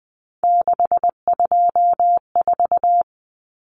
Morse code, heard as a keyed tone, about 20 wpm, 700 Hz. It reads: 624
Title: Numbers - 3-Digit 20wpm